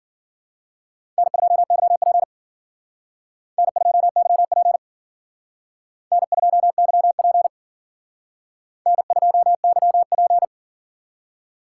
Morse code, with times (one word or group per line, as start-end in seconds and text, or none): 1.2-2.2 N2YP
3.6-4.8 N2YP
6.1-7.5 N2YP
8.9-10.4 N2YP